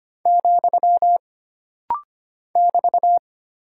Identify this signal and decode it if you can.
Morse code, heard as a keyed tone, about 25 wpm, 700 Hz. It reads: , <BT>
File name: Single Punctuation-Prosign - Rapid-Fire - 25wpm